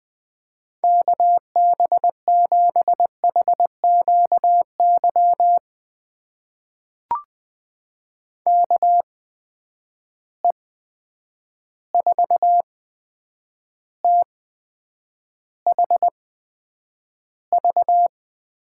Morse code, heard as a keyed tone, about 20 wpm, spaced as for 9 wpm, 700 Hz. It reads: KB7HQY KE4THV
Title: USA Call Signs - 6 Characters - 8x Character Spacing - 20wpm